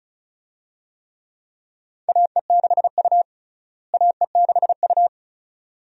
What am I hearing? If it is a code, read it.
Morse code, heard as a keyed tone, about 34 wpm, 700 Hz. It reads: AE6U AE6U
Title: USA Call Signs - 4 Characters - Repeated 2x - 35wpm